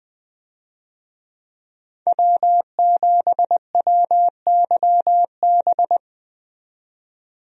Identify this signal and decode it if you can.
Morse code, heard as a keyed tone, about 20 wpm, 700 Hz. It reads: W7WYB
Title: USA Call Signs - 5 Characters - 20wpm